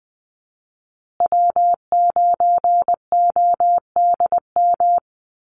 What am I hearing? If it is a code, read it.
Morse code, heard as a keyed tone, about 20 wpm, 700 Hz. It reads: W9ODM
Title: USA Call Signs Repeated 3x 20wpm